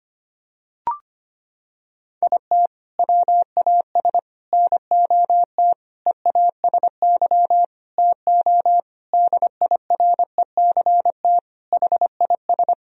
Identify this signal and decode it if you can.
Morse code, heard as a keyed tone, about 25 wpm, 700 Hz. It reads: IT WAS NOT EASY TO DIRECT HIS
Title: Sentences from Top 1000 Words - Review Part 2 - 25wpm